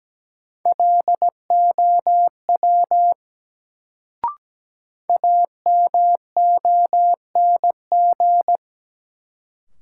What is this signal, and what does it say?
Morse code, heard as a keyed tone, about 17 wpm, 700 Hz. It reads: LOW AMONG
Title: Sentences from Top 500 Words - New Word Drill - 17wpm